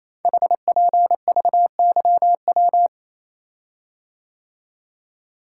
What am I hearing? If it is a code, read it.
Morse code, heard as a keyed tone, about 27 wpm, 700 Hz. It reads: HPVYW
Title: Code Groups - 5 Letters - 28wpm